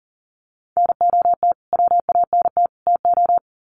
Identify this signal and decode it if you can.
Morse code, heard as a keyed tone, about 40 wpm, 700 Hz. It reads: NOT WANT TO
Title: Sets of 3 Words 40wpm